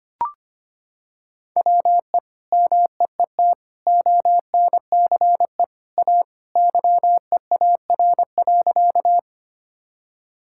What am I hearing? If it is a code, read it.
Morse code, heard as a keyed tone, about 25 wpm, 700 Hz. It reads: WE MEET ONCE A YEAR.
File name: practice / 25wpm / Sentences with 2-Word Phrases - 25wpm